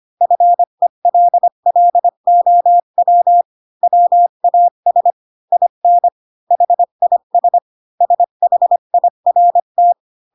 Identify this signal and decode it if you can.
Morse code, heard as a keyed tone, about 25 wpm, 700 Hz. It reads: FELLOW WAS IN HIS SHIRT
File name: Pinocchio / Pinocchio ch8 20 25 ik1hge